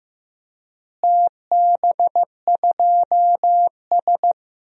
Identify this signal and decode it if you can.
Morse code, heard as a keyed tone, about 15 wpm, 700 Hz. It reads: TB2S
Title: DXCC-100 Call Signs - 4 Characters - Repeated 3x - 15wpm